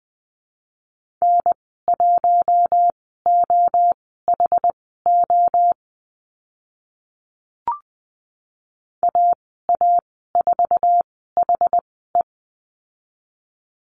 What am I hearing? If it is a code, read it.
Morse code, heard as a keyed tone, about 20 wpm, spaced as for 14 wpm, 700 Hz. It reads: N1OHO AA4HE
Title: 062 Lesson 11g - Letter h - Call Signs 20wpm